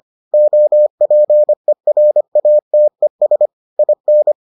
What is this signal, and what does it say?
Morse code, heard as a keyed tone, about 25 wpm, 600 Hz. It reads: OPERATES IN